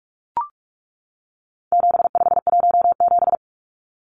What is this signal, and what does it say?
Morse code, heard as a keyed tone, about 44 wpm, 700 Hz. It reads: 7517
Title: Numbers - 4-Digit 45wpm